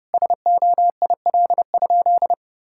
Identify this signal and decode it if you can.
Morse code, heard as a keyed tone, about 30 wpm, 700 Hz. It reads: SOIL?
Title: Sentences from Top 1600 Words 30wpm